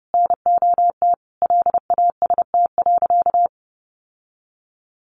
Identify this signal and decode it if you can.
Morse code, heard as a keyed tone, about 30 wpm, 700 Hz. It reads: NOT LAST.
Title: Sentences from Top 1200 Words 30wpm